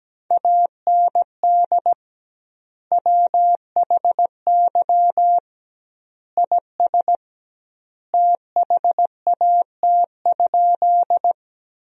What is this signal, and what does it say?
Morse code, heard as a keyed tone, about 17 wpm, 700 Hz. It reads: AND WHY IS THAT?
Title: 2x Word-Spacing - Sentences from Top 200 Words - 17wpm